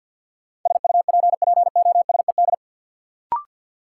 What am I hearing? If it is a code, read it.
Morse code, heard as a keyed tone, about 52 wpm, 700 Hz. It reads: SUPPOSED
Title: Sets of 3 Words - New Word Drill - 50wpm